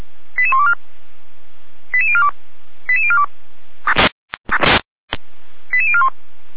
Five-tone selective call (ZVEI), heard as quick runs of five tones, tones of about 70 ms. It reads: 88124 88042 88042 88041